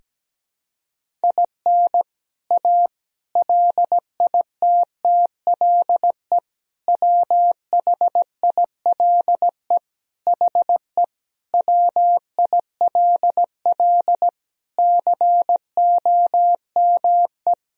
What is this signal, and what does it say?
Morse code, heard as a keyed tone, about 17 wpm, 700 Hz. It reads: IN A LITTLE WHILE HE WILL COME
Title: Sentences from Top 1000 Words - Review Part 1 - 17wpm